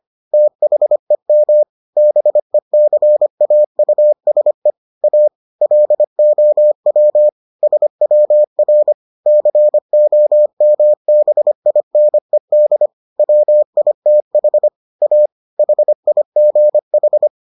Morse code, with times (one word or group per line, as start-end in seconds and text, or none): 0.3-1.6 THEM
2.0-4.7 BECAUSE
5.0-5.3 A
5.6-7.3 LOW
7.6-8.9 SWR
9.3-12.9 COMBINED
13.2-14.7 WITH
15.0-15.3 A
15.6-17.3 HIGH